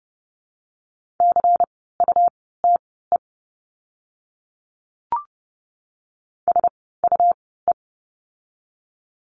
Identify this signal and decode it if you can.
Morse code, heard as a keyed tone, about 30 wpm, spaced as for 17 wpm, 700 Hz. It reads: CUTE SUE
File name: Lesson 17f - Letter u - Words 30wpm